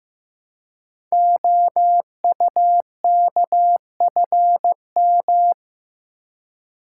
Morse code, heard as a keyed tone, about 15 wpm, 690 Hz. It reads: OUKFM
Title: Code Groups - 5 Letters - 15wpm